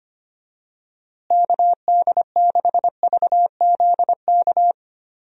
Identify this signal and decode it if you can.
Morse code, heard as a keyed tone, about 25 wpm, 700 Hz. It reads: KD6VZK
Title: USA Call Signs - 6 Characters - Repeated 3x - 25wpm